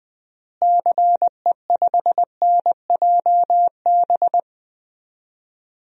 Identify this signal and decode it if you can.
Morse code, heard as a keyed tone, about 21 wpm, 700 Hz. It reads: CE5NJB
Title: DXCC-100 Call Signs - 6 Characters - Repeated 3x - 20wpm